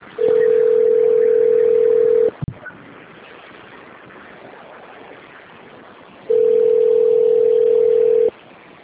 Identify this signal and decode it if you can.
Ringback tone (North America), 440 + 480 Hz.